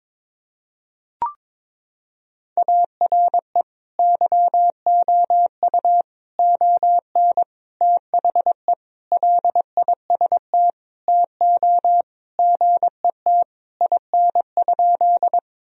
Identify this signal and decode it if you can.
Morse code, heard as a keyed tone, about 22 wpm, 700 Hz. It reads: ARE YOU ON THE LIST TO GET IN?
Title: Sentences from Top 1000 Words - Review Part 2 - 22wpm